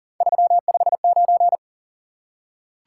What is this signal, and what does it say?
Morse code, heard as a keyed tone, about 38 wpm, 700 Hz. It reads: 359
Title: Top 180 QSO Elements 40wpm